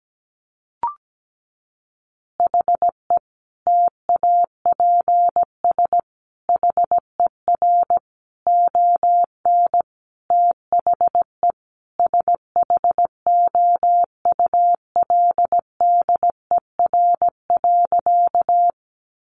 Morse code, heard as a keyed tone, about 17 wpm, 700 Hz. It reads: HE TAPS HER ON THE SHOULDER.
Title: Additional Easy Phrases 17wpm